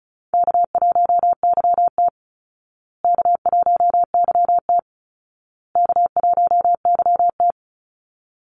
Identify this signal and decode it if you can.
Morse code, heard as a keyed tone, about 34 wpm, 700 Hz. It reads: K1YT K1YT K1YT
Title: Easy Call Signs Repeated 3x - 35wpm